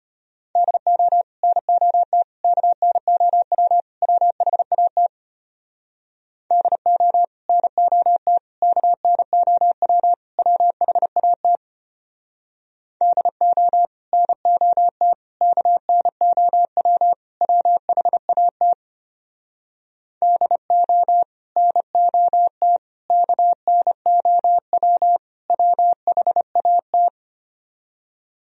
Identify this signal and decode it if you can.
Morse code, heard as a keyed tone, about 31 wpm, 700 Hz. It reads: DO NOT KNOW WHAT DO NOT KNOW WHAT DO NOT KNOW WHAT DO NOT KNOW WHAT
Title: Speed-Racing - Sets of 4 Words 25wpm